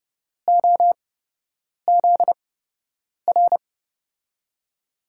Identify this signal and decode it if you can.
Morse code, heard as a keyed tone, about 30 wpm, spaced as for 8 wpm, 700 Hz. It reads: OZR
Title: USA Call Signs - 6 Characters - 8x Character Spacing - 30wpm